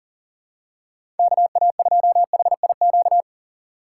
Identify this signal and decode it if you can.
Morse code, heard as a keyed tone, about 40 wpm, 700 Hz. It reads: KA2HIQ
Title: USA Call Signs - 6 Characters - 4x Character Spacing - 40wpm